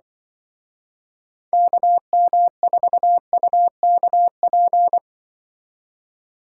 Morse code, heard as a keyed tone, about 25 wpm, 700 Hz. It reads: KM4UKP
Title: Speed Racing - USA 2x3 Call Signs 20wpm